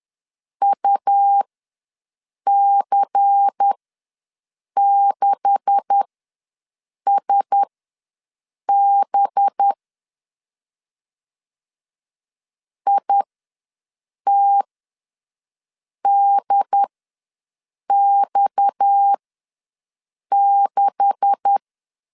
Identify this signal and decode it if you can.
Morse code, heard as a keyed tone, about 10 wpm, spaced as for 6 wpm, 800 Hz. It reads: UC6SB ITDX6
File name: LEZIONI / 9 286 ws 10wpm 4032